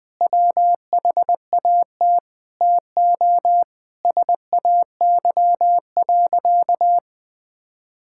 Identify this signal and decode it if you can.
Morse code, heard as a keyed tone, about 20 wpm, 700 Hz. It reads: WHAT TO SAY.
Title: Sentences with 5-Word Phrases - 20wpm